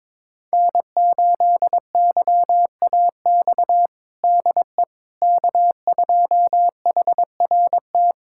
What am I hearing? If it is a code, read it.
Morse code, heard as a keyed tone, about 22 wpm, 700 Hz. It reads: N8YAX DE K2HRT
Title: USA Callsign DE Callsign - 5 Characters - 22wpm